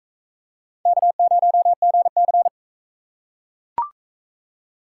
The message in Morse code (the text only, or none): K0GC